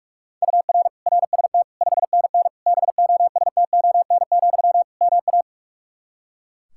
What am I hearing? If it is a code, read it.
Morse code, heard as a keyed tone, about 42 wpm, 700 Hz. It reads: UR RST 5NN BOSTON, MA
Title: Practice QSO - Exchange 1 - Beginning - 45wpm